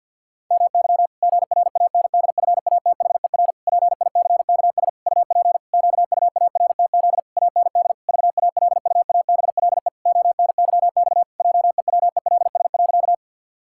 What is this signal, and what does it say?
Morse code, heard as a keyed tone, about 51 wpm, 700 Hz. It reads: MY GRANDFATHER PICKS UP QUARTZ AND VALUABLE ONYX JEWELS.
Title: Pangram Challenge v2 - 50wpm